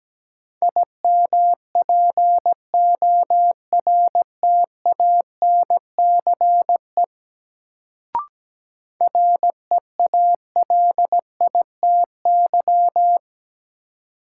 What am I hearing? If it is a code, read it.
Morse code, heard as a keyed tone, about 17 wpm, 700 Hz. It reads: IMPORTANCE REALITY
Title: Sentences from Top 1200 Words - New Word Drill - 17wpm